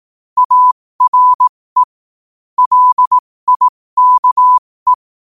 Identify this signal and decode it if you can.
Morse code, heard as a keyed tone, about 18 wpm, spaced as for 15 wpm, 1000 Hz. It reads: ARE LIKE
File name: qotdfiles / 15wpm